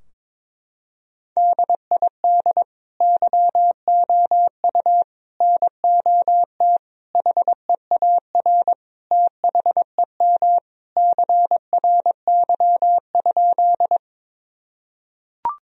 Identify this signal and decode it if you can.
Morse code, heard as a keyed tone, about 22 wpm, 700 Hz. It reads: DID YOU NOT HEAR THEM CRY?
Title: Sentences from Top 1000 Words - Review Part 2 - 22wpm